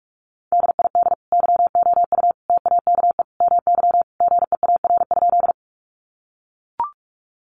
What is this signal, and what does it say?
Morse code, heard as a keyed tone, about 44 wpm, 700 Hz. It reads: DID YOU TAKE MY GEAR?
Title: Sentences from Top 1400 Words 45wpm